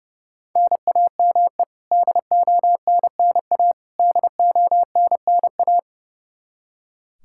Morse code, heard as a keyed tone, about 30 wpm, 700 Hz. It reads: NAME DONNA DONNA
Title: Practice QSO - Exchange 1 - Beginning - 30wpm